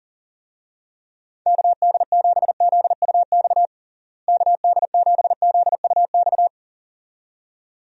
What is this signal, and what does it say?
Morse code, heard as a keyed tone, about 40 wpm, 700 Hz. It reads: KD7ZUX KD7ZUX
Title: USA Call Signs - 6 Characters - Repeated 2x - 40wpm